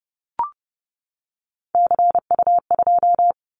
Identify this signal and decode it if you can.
Morse code, heard as a keyed tone, about 30 wpm, 700 Hz. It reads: CU2
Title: Callsign Prefixes for Top 50 DXCC - 30wpm